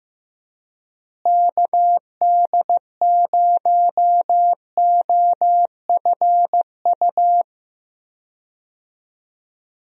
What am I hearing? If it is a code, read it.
Morse code, heard as a keyed tone, about 15 wpm, 700 Hz. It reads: KD0OFU
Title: USA Call Signs - 6 Characters - Repeated 2x - 15wpm